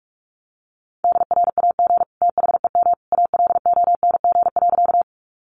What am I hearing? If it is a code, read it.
Morse code, heard as a keyed tone, about 44 wpm, 700 Hz. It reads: DRAG THEM ALONG.
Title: Sentences from Top 1500 Words 45wpm